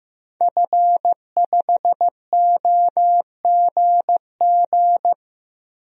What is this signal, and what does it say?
Morse code, heard as a keyed tone, about 16 wpm, 700 Hz. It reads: F5OGG
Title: DXCC-100 Call Signs - 5 Characters - 15wpm